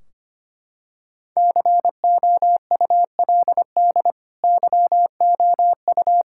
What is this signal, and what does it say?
Morse code, heard as a keyed tone, about 25 wpm, 700 Hz. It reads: COULD YOU
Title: Sentences from Top 1600 Words 25wpm